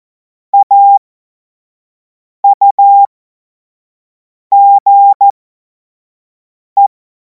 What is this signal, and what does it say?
Morse code, heard as a keyed tone, about 14 wpm, spaced as for 5 wpm, 800 Hz. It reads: AUGE
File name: Morsekurs / Witze / Witz 03. Februar 2026 25 F70